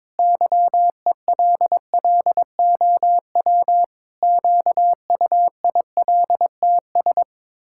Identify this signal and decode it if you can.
Morse code, heard as a keyed tone, about 22 wpm, 700 Hz. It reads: YELLOW QUILTS
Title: Pangram Challenge v2 - 22wpm